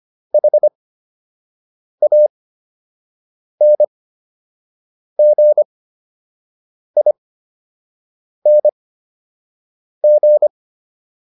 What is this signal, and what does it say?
Morse code, heard as a keyed tone, about 25 wpm, spaced as for 6 wpm, 600 Hz. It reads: HANGING